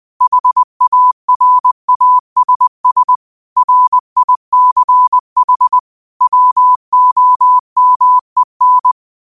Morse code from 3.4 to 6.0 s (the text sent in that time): RICH